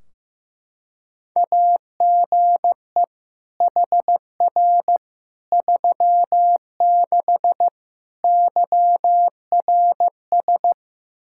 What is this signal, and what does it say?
Morse code, heard as a keyed tone, about 15 wpm, 700 Hz. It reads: AGE HR 36 YRS
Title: Practice QSO - Exchange 3 - More Info - 15wpm